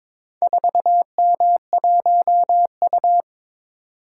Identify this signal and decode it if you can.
Morse code, heard as a keyed tone, about 22 wpm, 700 Hz. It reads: 4M1U
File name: practice / 22wpm / Four Letter-Number Combinations - 22wpm